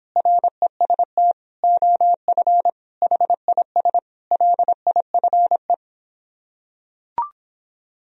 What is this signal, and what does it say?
Morse code, heard as a keyed tone, about 26 wpm, 700 Hz. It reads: REST OF HIS LIFE